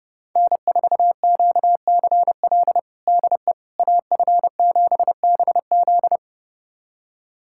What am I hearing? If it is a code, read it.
Morse code, heard as a keyed tone, about 30 wpm, 700 Hz. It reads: N4QCL DE AF7BZ
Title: USA Callsign DE Callsign - 5 Characters - 30wpm